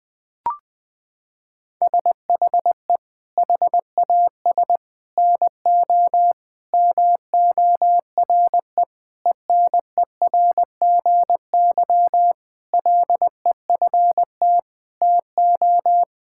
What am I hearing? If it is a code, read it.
Morse code, heard as a keyed tone, about 21 wpm, 700 Hz. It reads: SHE HAS NO MORE ENERGY LEFT TO
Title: Sentences from Top 1000 Words - Review Part 6 - 20wpm